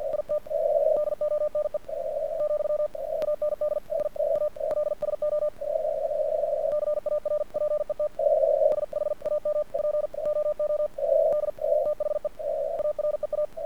Morse code, heard as a keyed tone, about 48 wpm, spaced as for 41 wpm, 1200 Hz. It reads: IT DONE , AND I ALSO CARPET SHAMPOOD THE AREA
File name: archived / CRSnet